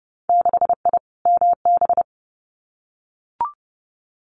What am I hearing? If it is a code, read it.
Morse code, heard as a keyed tone, about 30 wpm, 700 Hz. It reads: 6I MB
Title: Winter Field Day Exchanges Encore 30wpm